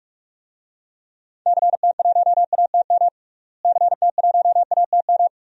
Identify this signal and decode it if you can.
Morse code, heard as a keyed tone, about 48 wpm, 700 Hz. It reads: CT1ATM CT1ATM
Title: DXCC-100 Call Signs - 6 Characters - Repeated 2x - 45wpm